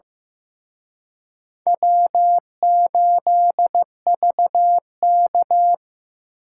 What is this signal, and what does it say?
Morse code, heard as a keyed tone, about 18 wpm, 700 Hz. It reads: W8VK
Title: Speed Racing - USA 1x2 Call Signs 15wpm